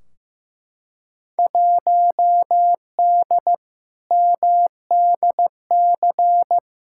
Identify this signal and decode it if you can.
Morse code, heard as a keyed tone, about 15 wpm, 700 Hz. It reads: 1D MDC
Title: Field Day Exchanges - 15wpm